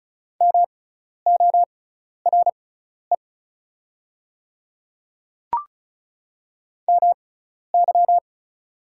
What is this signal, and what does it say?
Morse code, heard as a keyed tone, about 36 wpm, spaced as for 12 wpm, 690 Hz. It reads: MORE MY
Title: 6x Character Spacing - Top 100 Words - 35wpm